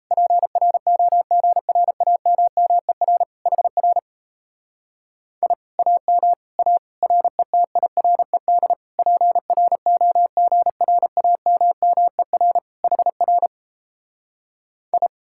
Morse code, heard as a keyed tone, about 37 wpm, 700 Hz. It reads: PROGRAMMER HR I AM A RETIRED PROGRAMMER HR I